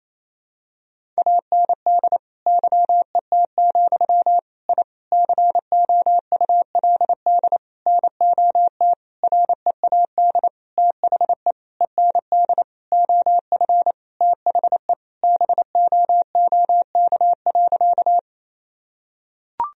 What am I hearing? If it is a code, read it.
Morse code, heard as a keyed tone, about 28 wpm, 700 Hz. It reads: AND YET, I COULD NOT READ THE END OF THE BOOK.
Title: Sentences from Top 1000 Words - Review Part 1 - 28wpm